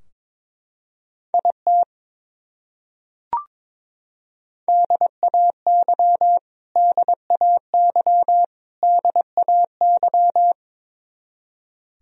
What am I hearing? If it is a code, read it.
Morse code, heard as a keyed tone, about 22 wpm, 700 Hz. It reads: IT DAY DAY DAY
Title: Top 100 Words Repeated 22wpm